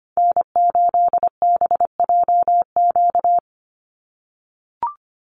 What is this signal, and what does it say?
Morse code, heard as a keyed tone, about 25 wpm, 700 Hz. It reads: N8BJQ
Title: Call Signs - 25wpm